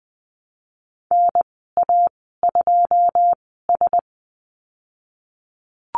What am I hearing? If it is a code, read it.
Morse code, heard as a keyed tone, about 20 wpm, spaced as for 15 wpm, 700 Hz. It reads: NA2S
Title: 082 Lesson 14g - Number 2 - Call Signs 20wpm